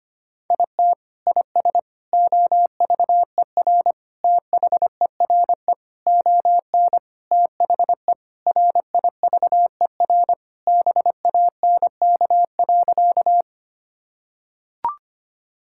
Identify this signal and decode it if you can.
Morse code, heard as a keyed tone, about 25 wpm, 700 Hz. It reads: IT IS OVER THERE ON THE RIVER BANK.
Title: Sentences from Top 1000 Words - Review Part 6 - 25wpm